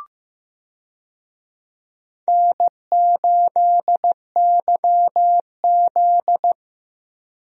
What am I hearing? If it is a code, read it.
Morse code, heard as a keyed tone, about 15 wpm, 700 Hz. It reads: N8YZ
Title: USA Call Signs - 4 Characters - Repeated 3x - 15wpm